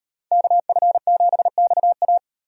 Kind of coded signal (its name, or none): Morse code